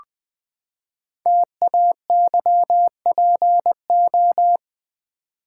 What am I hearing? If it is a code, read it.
Morse code, heard as a keyed tone, about 20 wpm, 700 Hz. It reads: TAYPO
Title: Code Groups - 5 Letters - 20wpm